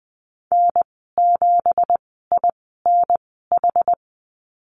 Morse code, heard as a keyed tone, about 20 wpm, spaced as for 15 wpm, 700 Hz. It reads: N7INH
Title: 186 Lesson 30g - Letter b - Call Signs 20wpm